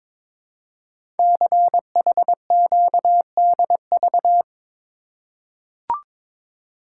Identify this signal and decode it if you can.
Morse code, heard as a keyed tone, about 22 wpm, 700 Hz. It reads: CHQDV
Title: Code Groups - 5 Letter:Numbers - 22wpm